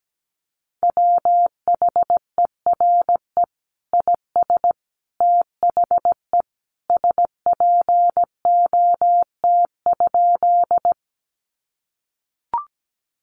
Morse code, heard as a keyed tone, about 17 wpm, 700 Hz. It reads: WHERE IS THE SPOT?
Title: Sentences from Top 1100 Words 17wpm